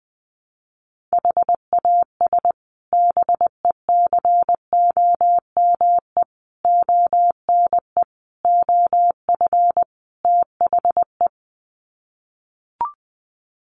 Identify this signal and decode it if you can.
Morse code, heard as a keyed tone, about 20 wpm, 700 Hz. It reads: HAS BECOME ONE OF THE
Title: Sets of 5 Words Encore 20wpm